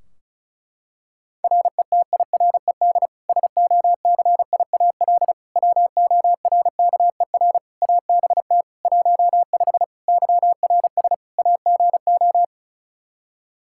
Morse code, heard as a keyed tone, about 36 wpm, 700 Hz. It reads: RETIRED SOCIAL WORKER ABT 15 YRS AGO
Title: Practice QSO - Exchange 3 - More Info - 35wpm